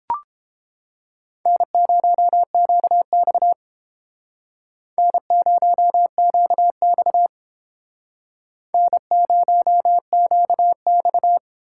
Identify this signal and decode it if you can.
Morse code, heard as a keyed tone, about 30 wpm, 700 Hz. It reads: N0QX N0QX N0QX